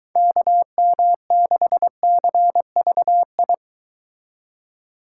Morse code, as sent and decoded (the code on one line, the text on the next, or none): -.- -- -.... -.-. ...- ..
KM6CVI